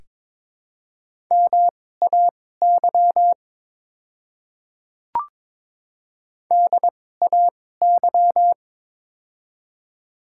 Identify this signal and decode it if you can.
Morse code, heard as a keyed tone, about 22 wpm, spaced as for 16 wpm, 700 Hz. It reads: MAY DAY
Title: 2x Character Spacing - Top 100 Words - 22wpm